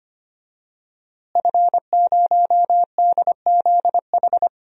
Morse code, heard as a keyed tone, about 25 wpm, 700 Hz. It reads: F0DZH
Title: DXCC-100 Call Signs - 5 Characters - Repeated 2x - 25wpm